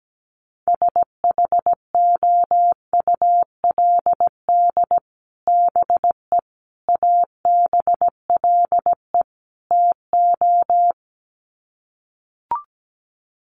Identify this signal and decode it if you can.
Morse code, heard as a keyed tone, about 17 wpm, 700 Hz. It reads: SHOULD BE ABLE TO
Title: Sets of 4 Words 17wpm